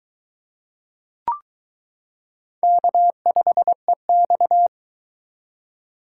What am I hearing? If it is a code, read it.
Morse code, heard as a keyed tone, about 22 wpm, 700 Hz. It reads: K5EX